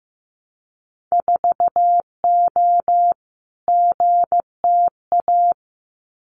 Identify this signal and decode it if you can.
Morse code, heard as a keyed tone, about 15 wpm, 700 Hz. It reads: 4O GTA
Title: Winter Field Day Exchanges 15wpm